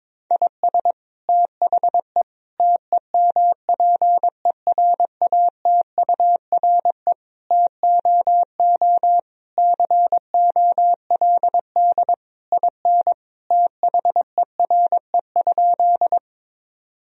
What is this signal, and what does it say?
Morse code, heard as a keyed tone, about 22 wpm, 700 Hz. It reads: IS THE TEMPERATURE TOO COLD IN THERE?
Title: Sentences from Top 1000 Words - Review Part 4 - 22wpm